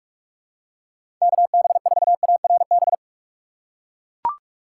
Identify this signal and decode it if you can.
Morse code, heard as a keyed tone, about 42 wpm, 700 Hz. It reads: KB4ARB
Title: USA Call Signs - 4x Character Spacing - 45wpm